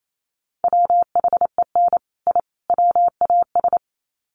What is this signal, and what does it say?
Morse code, heard as a keyed tone, about 28 wpm, 700 Hz. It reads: WHEN I WAS
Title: Sets of 3 Words 28wpm